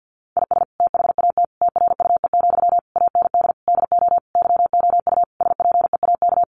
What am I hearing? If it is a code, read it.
Morse code, heard as a keyed tone, about 50 wpm, 700 Hz. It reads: IS THAT TRUE, AND DO YOU SPEAK
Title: Sentences from Top 1200 Words 50wpm